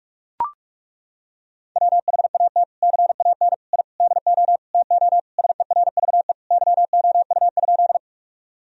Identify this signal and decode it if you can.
Morse code, heard as a keyed tone, about 42 wpm, 700 Hz. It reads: WHAT CAN I DO TO SERVE YOU?
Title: Sentences from Top 1000 Words - Review Part 2 - 45wpm